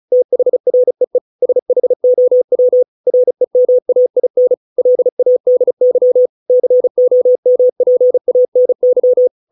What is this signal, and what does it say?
Morse code, heard as a keyed tone, about 34 wpm, 500 Hz. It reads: THREE SHOW REMAIN LADY COMPANY